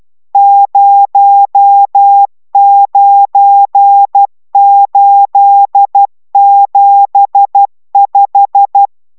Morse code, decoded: 09875

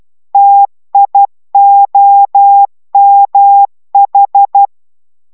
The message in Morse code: TIOMH

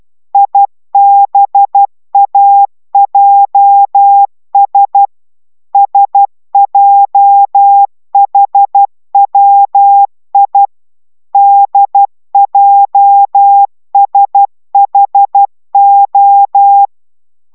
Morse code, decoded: IBAJS SJHWI DJSHO

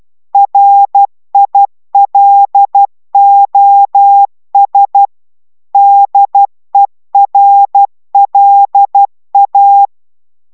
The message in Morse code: RILOS DERLA

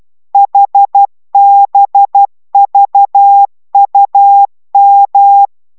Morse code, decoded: HBVUM

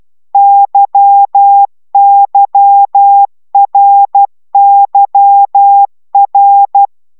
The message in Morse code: YYRYR

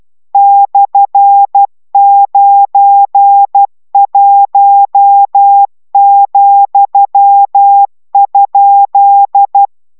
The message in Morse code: /91,?

